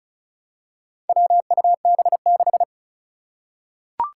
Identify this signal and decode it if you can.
Morse code, heard as a keyed tone, about 36 wpm, 700 Hz. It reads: WUB6
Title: Four Challenging Letter-Number Combinations 35wpm